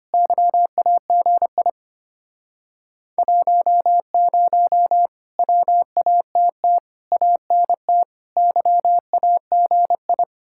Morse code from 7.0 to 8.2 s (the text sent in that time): ANT